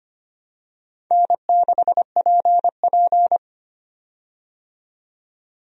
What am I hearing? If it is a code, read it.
Morse code, heard as a keyed tone, about 25 wpm, 700 Hz. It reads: N6PP
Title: USA Call Signs - 4 Characters - 25wpm